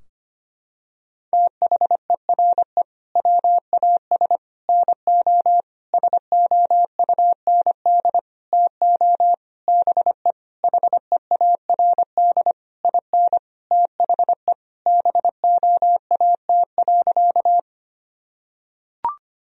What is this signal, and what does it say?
Morse code, heard as a keyed tone, about 25 wpm, 700 Hz. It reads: THERE WAS NO SOUND TO BE HEARD IN THE BOAT.
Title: Sentences from Top 1000 Words - Review Part 3 - 25wpm